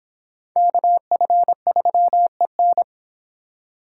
Morse code, heard as a keyed tone, about 29 wpm, 700 Hz. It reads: KF3EN